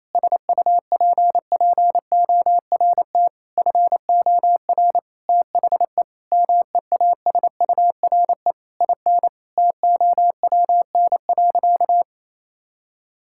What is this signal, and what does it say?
Morse code, heard as a keyed tone, about 28 wpm, 700 Hz. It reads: SUPPORT FOR THE MEASURE IN TOWN.
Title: Sentences from Top 1000 Words - Review Part 5 - 28wpm